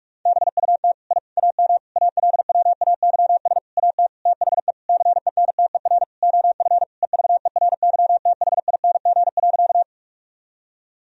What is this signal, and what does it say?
Morse code, heard as a keyed tone, about 41 wpm, 700 Hz. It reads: BUT I AM ALWAYS AT THE CENTER OF EVERYTHING.
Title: Sentences from Top 1000 Words - Review Part 2 - 45wpm